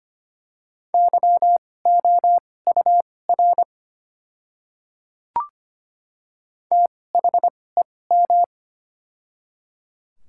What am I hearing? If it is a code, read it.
Morse code, heard as a keyed tone, about 25 wpm, spaced as for 18 wpm, 700 Hz. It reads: YOUR THEM
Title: 2x Character Spacing - Top 100 Words - 25wpm